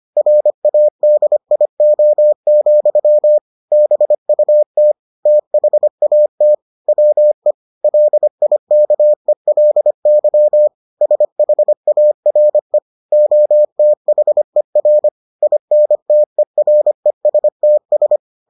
Morse code, heard as a keyed tone, about 25 wpm, 600 Hz. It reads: RADIO, BUT THAT WE LIKELY SHARE OTHER INTERESTS